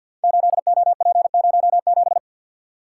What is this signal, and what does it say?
Morse code, heard as a keyed tone, about 52 wpm, 690 Hz. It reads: 8OP07
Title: Code Groups - 5 Letter:Numbers - 50wpm